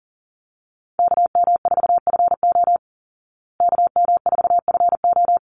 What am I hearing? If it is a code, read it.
Morse code, heard as a keyed tone, about 40 wpm, 700 Hz. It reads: KM4FO KM4FO
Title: Call Signs Repeated 2x - 40wpm